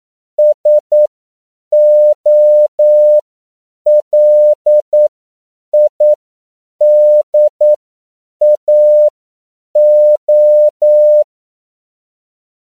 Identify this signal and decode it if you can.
Morse code, heard as a keyed tone, about 9 wpm, spaced as for 7 wpm, 600 Hz. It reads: SOLIDAO
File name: CW / 10PPM